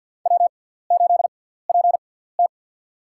Morse code, heard as a keyed tone, about 53 wpm, spaced as for 17 wpm, 700 Hz. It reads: W8PT
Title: USA Call Signs - 4 Characters - 6x Character Spacing - 50wpm